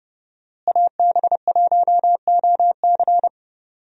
Morse code, heard as a keyed tone, about 30 wpm, 700 Hz. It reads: AB1OC
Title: USA Call Signs - 5 Characters - Repeated 3x - 30wpm